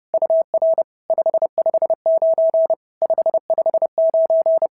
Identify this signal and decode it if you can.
Morse code, heard as a keyed tone, about 30 wpm, 650 Hz. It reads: UR 559 559
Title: SOTA QSO Practice 30wpm